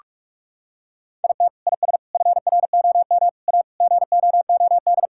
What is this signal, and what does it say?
Morse code, heard as a keyed tone, about 42 wpm, 700 Hz. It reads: IT IS FROM A GOOD
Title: Sentences from Top 1000 Words - Review Part 5 - 45wpm